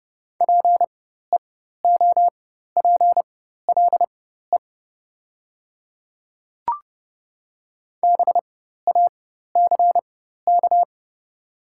Morse code, heard as a keyed tone, about 30 wpm, spaced as for 14 wpm, 700 Hz. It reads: PEOPLE BACK
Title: 4x Character Spacing - Top 200 Words - 30wpm